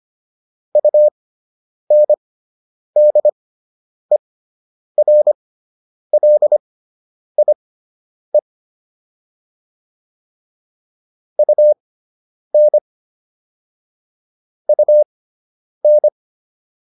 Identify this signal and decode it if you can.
Morse code, heard as a keyed tone, about 25 wpm, spaced as for 9 wpm, 600 Hz. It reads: UNDERLIE UN UN